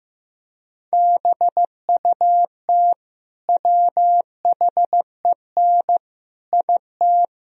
Morse code, heard as a keyed tone, about 15 wpm, 700 Hz. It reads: BUT WHEN IT
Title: Speed-Racing - Sets of 5 Words 15wpm